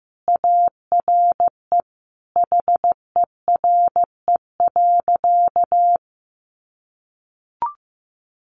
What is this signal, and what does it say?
Morse code, heard as a keyed tone, about 15 wpm, 700 Hz. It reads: ARE HERE.
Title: Sentences from Top 200 Words 15wpm